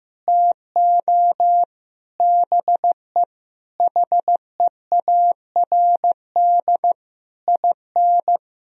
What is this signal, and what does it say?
Morse code, heard as a keyed tone, about 15 wpm, 700 Hz. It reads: TO BE HEARD IN
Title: Sentences from Top 1000 Words - Review Part 3 - 15wpm